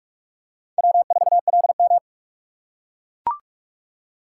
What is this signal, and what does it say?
Morse code, heard as a keyed tone, about 44 wpm, 700 Hz. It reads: W4LM